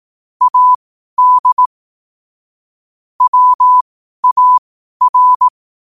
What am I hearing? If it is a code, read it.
Morse code, heard as a keyed tone, about 18 wpm, spaced as for 12 wpm, 1000 Hz. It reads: AD WAR
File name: qotdfiles / 12wpm